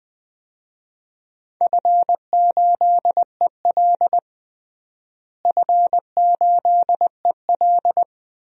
Morse code, heard as a keyed tone, about 20 wpm, 700 Hz. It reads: F8EL F8EL
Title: DXCC-100 Call Signs - 4 Characters - Repeated 3x - 20wpm